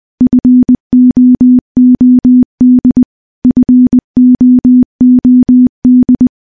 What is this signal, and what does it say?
Morse code, heard as a keyed tone, about 20 wpm, 250 Hz. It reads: FOOD FOOD
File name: morse / morse-training-20WPM